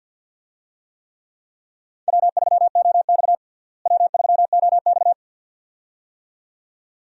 Morse code, heard as a keyed tone, about 52 wpm, 700 Hz. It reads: W3OX W3OX
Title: USA Call Signs - 4 Characters - Repeated 2x - 50wpm